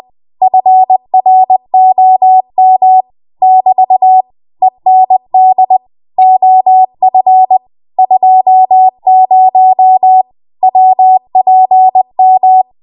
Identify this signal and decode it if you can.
Morse code, heard as a keyed tone, about 20 wpm, 750 Hz. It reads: FROM <BT> END OF 20 WPM
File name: Morse / 20WPM